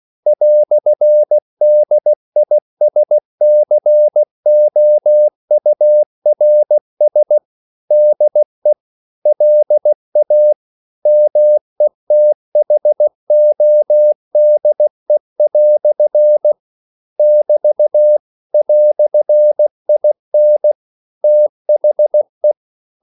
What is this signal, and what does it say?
Morse code, heard as a keyed tone, about 16 wpm, 600 Hz. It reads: "DISCOURS DE LA METHODE" <BT> "IN THE
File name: CW / Practice / EnglishQuotations0116